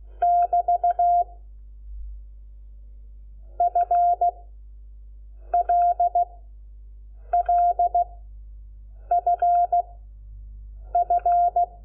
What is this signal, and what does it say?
Morse code, heard as a keyed tone, about 16 wpm, 690 Hz. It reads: <BT> F L L F F
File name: Lecciones / CLASE Nº5